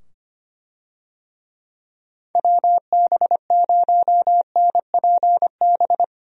Morse code, heard as a keyed tone, about 25 wpm, 700 Hz. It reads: WB0NPB
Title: USA Call Signs - 6 Characters - 25wpm